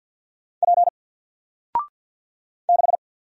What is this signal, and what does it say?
Morse code, heard as a keyed tone, about 50 wpm, 700 Hz. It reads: P 6
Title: Single Letter-Number 50wpm